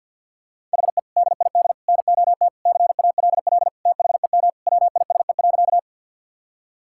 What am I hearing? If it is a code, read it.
Morse code, heard as a keyed tone, about 52 wpm, 700 Hz. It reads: HE DID NOT CALL THEM WISE.
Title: Sentences from Top 1600 Words 50wpm